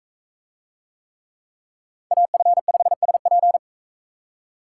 Morse code, heard as a keyed tone, about 40 wpm, 700 Hz. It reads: AF5SP